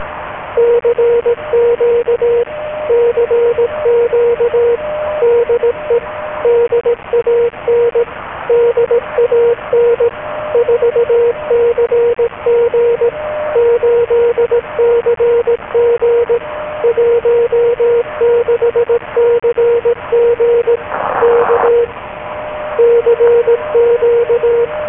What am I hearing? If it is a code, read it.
Morse code, heard as a keyed tone, about 18 wpm, 470 Hz, with radio noise. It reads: CQ CQ DE DAN DAN 4CG 8CG 16CG K CQ
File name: DAN-Audio / CQ-A1